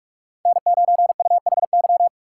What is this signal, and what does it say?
Morse code, heard as a keyed tone, about 41 wpm, 700 Hz. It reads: N9UHY
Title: USA Call Signs - 5 Characters - 2x Character Spacing - 45wpm